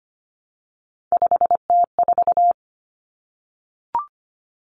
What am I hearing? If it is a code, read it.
Morse code, heard as a keyed tone, about 25 wpm, 700 Hz. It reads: 5T4
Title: Common Cut Numbers - 3-Digit 25wpm